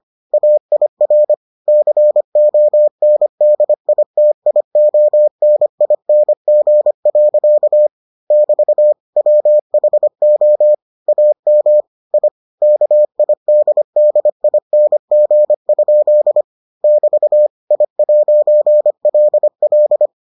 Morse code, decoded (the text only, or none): AIR CONDITIONING. <BT> WHO AM I KIDDING? <BT> I'LL